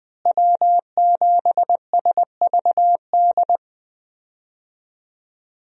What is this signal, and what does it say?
Morse code, heard as a keyed tone, about 21 wpm, 700 Hz. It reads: W7SVD
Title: USA Call Signs - 5 Characters - Repeated 2x - 20wpm